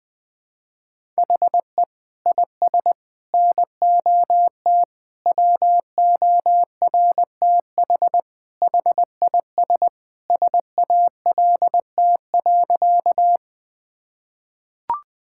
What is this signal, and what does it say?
Morse code, heard as a keyed tone, about 20 wpm, 700 Hz. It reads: HE IS NOT WORTH HIS SALT.
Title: Sentences from Top 1000 Words - Review Part 5 - 20wpm